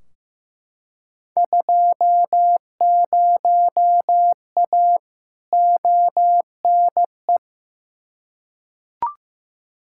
Morse code, heard as a keyed tone, about 15 wpm, 700 Hz. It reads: 20A ONE
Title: Field Day Exchanges Encore - 15wpm